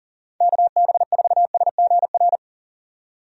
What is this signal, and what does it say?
Morse code, heard as a keyed tone, about 40 wpm, 700 Hz. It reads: KB4SGR
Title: USA Call Signs - 6 Characters - 40wpm